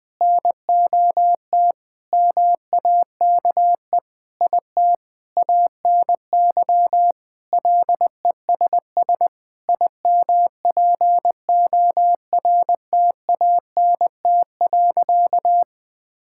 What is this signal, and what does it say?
Morse code, heard as a keyed tone, about 20 wpm, 700 Hz. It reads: NOT MAKE IT ANY LESS IMPORTANT.
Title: Sentences with 7-Word Phrases - 20wpm